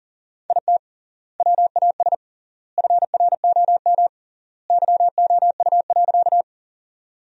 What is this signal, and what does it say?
Morse code, heard as a keyed tone, about 37 wpm, 700 Hz. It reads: IT WAS FROM YOU.
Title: 3x Word-Spacing - Sentences from Top 100 Words - 40wpm